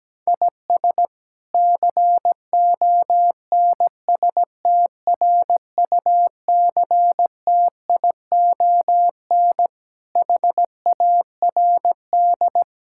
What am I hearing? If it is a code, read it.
Morse code, heard as a keyed tone, about 17 wpm, 700 Hz. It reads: IS CONSTRUCTION HARD
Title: Sentences from Top 1600 Words 17wpm